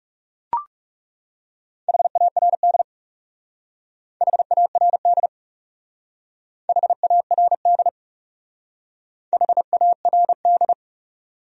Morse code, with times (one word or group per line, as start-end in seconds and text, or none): 1.9-2.8 HARD
4.2-5.3 HARD
6.7-7.9 HARD
9.3-10.7 HARD